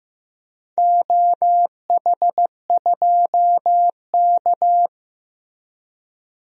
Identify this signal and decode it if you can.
Morse code, heard as a keyed tone, about 15 wpm, 700 Hz. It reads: OH2K
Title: DXCC-100 Call Signs - 4 Characters - Repeated 2x - 15wpm